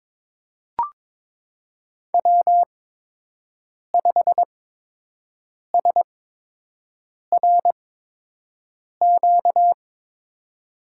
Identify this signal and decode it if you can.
Morse code, heard as a keyed tone, about 22 wpm, spaced as for 6 wpm, 700 Hz. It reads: W5SRQ
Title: USA Call Signs - 8x Character Spacing - 22wpm